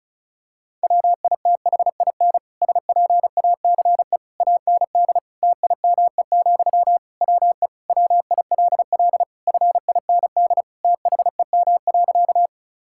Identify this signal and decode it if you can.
Morse code, heard as a keyed tone, about 34 wpm, 700 Hz. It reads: WITHIN SPACE AND TIME, WE WILL FIND THEM.
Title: Sentences from Top 1000 Words - Review Part 5 - 35wpm